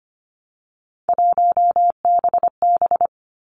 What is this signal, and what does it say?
Morse code, heard as a keyed tone, about 25 wpm, 700 Hz. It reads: 1BB
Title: Three Challenging Letter-Number Combinations 25wpm